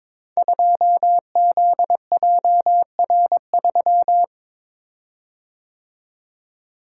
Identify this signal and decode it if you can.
Morse code, heard as a keyed tone, about 22 wpm, 700 Hz. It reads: 2ZJR3
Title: Code Groups - 5 Letter:Numbers - 22wpm